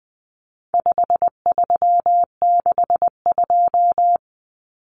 Morse code, heard as a keyed tone, about 20 wpm, 700 Hz. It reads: 5362
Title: Common Cut Numbers - 4-Digit 20wpm